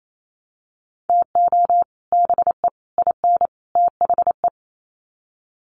Morse code, heard as a keyed tone, about 28 wpm, 700 Hz. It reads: TO BE IN THE
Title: Sets of 4 Words 28wpm